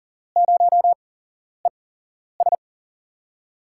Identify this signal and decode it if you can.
Morse code, heard as a keyed tone, about 40 wpm, spaced as for 11 wpm, 700 Hz. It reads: 0ES
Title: USA Call Signs - 8x Character Spacing - 40wpm